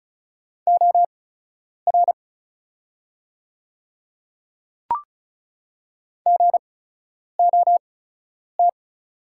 Morse code, heard as a keyed tone, about 36 wpm, spaced as for 10 wpm, 700 Hz. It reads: OR GOT
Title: 8x Character Spacing - Top 300 Words - 35wpm